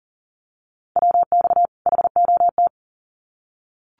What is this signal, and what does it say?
Morse code, heard as a keyed tone, about 40 wpm, 700 Hz. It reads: WX HOT
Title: Common QSO Phrases 40wpm